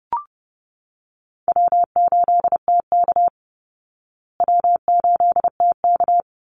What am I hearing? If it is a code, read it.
Morse code, heard as a keyed tone, about 30 wpm, 700 Hz. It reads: W8TK W8TK
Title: Call Signs Repeated 3x - 30wpm